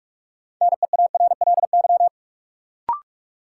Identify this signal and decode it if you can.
Morse code, heard as a keyed tone, about 43 wpm, 700 Hz. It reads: NEARLY
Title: Sentences from Top 1300 Words - New Word Drill Alt - 45wpm